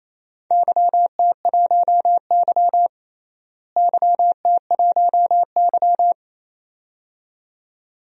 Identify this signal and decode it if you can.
Morse code, heard as a keyed tone, about 28 wpm, 700 Hz. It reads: YT1Y YT1Y
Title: DXCC-100 Call Signs - 4 Characters - Repeated 2x - 28wpm